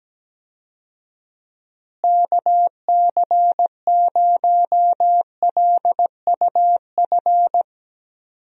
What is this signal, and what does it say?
Morse code, heard as a keyed tone, about 17 wpm, 700 Hz. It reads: KC0LUF
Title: USA Call Signs - 6 Characters - Repeated 2x - 17wpm